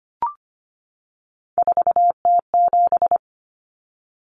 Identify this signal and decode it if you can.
Morse code, heard as a keyed tone, about 24 wpm, 700 Hz. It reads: 4T7
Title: Common Cut Numbers - 3-Digit 25wpm